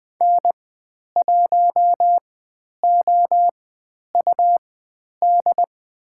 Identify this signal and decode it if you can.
Morse code, heard as a keyed tone, about 20 wpm, spaced as for 10 wpm, 700 Hz. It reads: N1OUD
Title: 102 Lesson 17g - Letter u - Call Signs 20wpm